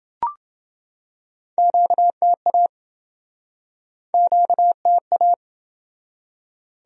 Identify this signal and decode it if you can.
Morse code, heard as a keyed tone, about 29 wpm, 700 Hz. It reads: QTA QTA